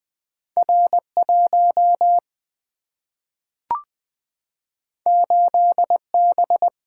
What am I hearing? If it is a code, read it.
Morse code, heard as a keyed tone, about 20 wpm, 700 Hz. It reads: R1 8B
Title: Single-Letter Single-Number 20wpm